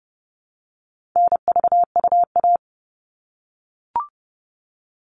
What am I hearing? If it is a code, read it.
Morse code, heard as a keyed tone, about 30 wpm, 700 Hz. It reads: NVUA
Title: All Cut Numbers - 4-Digit 30wpm